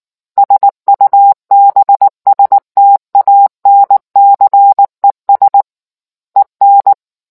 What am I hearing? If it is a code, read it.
Morse code, heard as a keyed tone, about 19 wpm, 800 Hz. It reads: SUBSTANCES EN